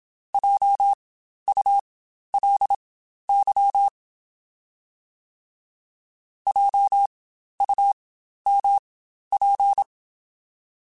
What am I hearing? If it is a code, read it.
Morse code, heard as a keyed tone, about 27 wpm, spaced as for 12 wpm, 800 Hz. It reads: JULY JUMP
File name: Spaced Alphas / Spaced Alphas mp3s / Js sp @25